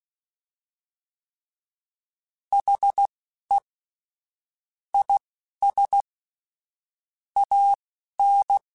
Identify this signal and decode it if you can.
Morse code, heard as a keyed tone, about 16 wpm, spaced as for 11 wpm, 800 Hz. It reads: HE IS AN
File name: Quotes / Quotes-mp3s / Optimist @10